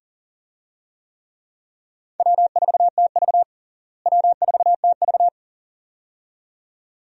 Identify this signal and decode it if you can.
Morse code, heard as a keyed tone, about 37 wpm, 700 Hz. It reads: W4TV W4TV
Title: USA Call Signs - 4 Characters - Repeated 2x - 40wpm